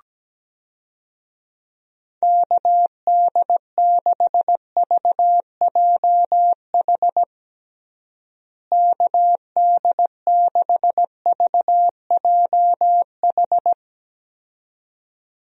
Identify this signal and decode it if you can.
Morse code, heard as a keyed tone, about 17 wpm, 700 Hz. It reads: KD6VJH KD6VJH
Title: USA Call Signs - 6 Characters - Repeated 2x - 17wpm